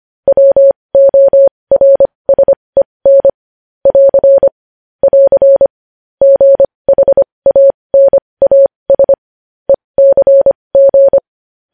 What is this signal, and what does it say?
Morse code, heard as a keyed tone, about 25 wpm, spaced as for 20 wpm, 550 Hz. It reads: WORSEN <AR> <AR> GHANAS ECG
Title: News Headlines at 20